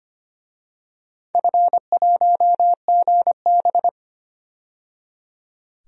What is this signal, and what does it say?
Morse code, heard as a keyed tone, about 25 wpm, 700 Hz. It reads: F1GB
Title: DXCC-100 Call Signs - 4 Characters - 25wpm